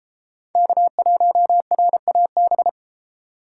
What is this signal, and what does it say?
Morse code, heard as a keyed tone, about 33 wpm, 700 Hz. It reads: K1RAB